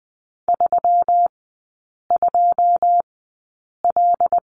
Morse code, heard as a keyed tone, about 20 wpm, spaced as for 8 wpm, 700 Hz. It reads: 32L
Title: 121 Lesson 20e - Number 3 - Three Characters 20wpm